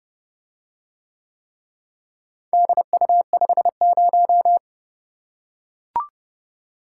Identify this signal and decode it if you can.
Morse code, heard as a keyed tone, about 29 wpm, 700 Hz. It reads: DU50